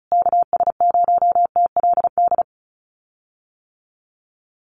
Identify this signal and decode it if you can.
Morse code, heard as a keyed tone, about 34 wpm, 700 Hz. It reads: KS0TLD
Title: USA Call Signs Repeated 3x 35wpm